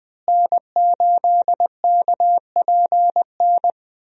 Morse code, heard as a keyed tone, about 20 wpm, 700 Hz. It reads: N8KPN
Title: USA Call Signs - 5 Characters - Repeated 3x - 20wpm